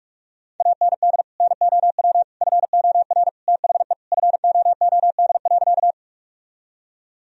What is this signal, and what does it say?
Morse code, heard as a keyed tone, about 46 wpm, 700 Hz. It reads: AND NOW FOR THE FOOD.
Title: Sentences from Top 1000 Words - Review Part 1 - 45wpm